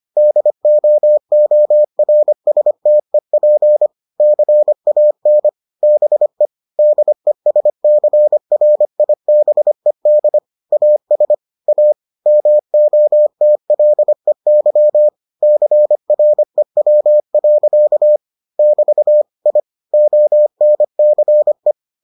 Morse code, heard as a keyed tone, about 25 wpm, 600 Hz. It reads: DOORSTEP CAN BE DESCRIBED AS A MOTLEY CREW. <BT> I ONCE